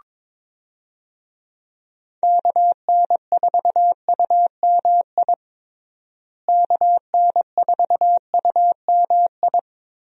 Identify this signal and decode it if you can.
Morse code, heard as a keyed tone, about 22 wpm, 700 Hz. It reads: KN4UMI KN4UMI
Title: USA Call Signs - 6 Characters - Repeated 3x - 22wpm